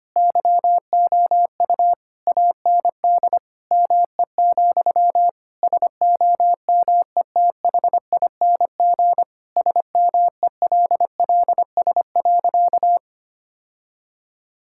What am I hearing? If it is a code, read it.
Morse code, heard as a keyed tone, about 25 wpm, 700 Hz. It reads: YOU AND ME, SOMETHING SMELLS.
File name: practice / 25wpm / All-Age Appropriate Jokes - 25wpm